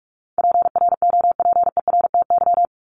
Morse code, heard as a keyed tone, about 44 wpm, 700 Hz. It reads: PROPERTY
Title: Top 751-1000 Words 45wpm